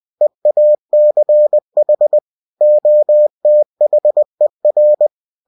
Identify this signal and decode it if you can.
Morse code, heard as a keyed tone, about 20 wpm, 600 Hz. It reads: EACH OTHER
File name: winnie-20wpm